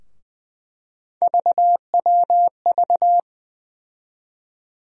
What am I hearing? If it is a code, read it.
Morse code, heard as a keyed tone, about 20 wpm, 700 Hz. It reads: VWV
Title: USA Call Sign Postfixes - 3 Letters 20wpm